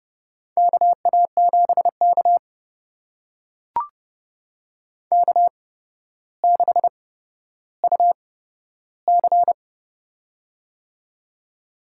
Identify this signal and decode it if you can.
Morse code, heard as a keyed tone, about 32 wpm, spaced as for 13 wpm, 700 Hz. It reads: KA7K K6UC
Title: USA Call Signs - 4 Characters - 8x Character Spacing - 30wpm